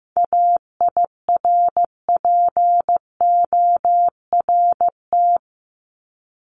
Morse code, heard as a keyed tone, about 15 wpm, 700 Hz. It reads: AIRPORT
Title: Top 1751-2000 Words 15wpm